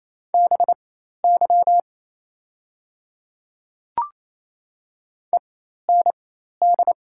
Morse code, heard as a keyed tone, about 27 wpm, spaced as for 13 wpm, 690 Hz. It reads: BY END
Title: 4x Character Spacing - Top 300 Words - 28wpm